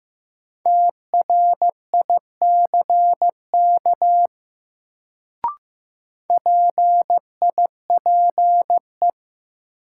Morse code, heard as a keyed tone, about 15 wpm, 700 Hz. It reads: TRICK PIPE
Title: Sentences from Top 1500 Words - New Word Drill - 15wpm